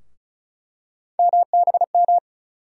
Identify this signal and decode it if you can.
Morse code, heard as a keyed tone, about 36 wpm, 700 Hz. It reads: MBM
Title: USA Call Sign Postfixes - 3 Letters 35wpm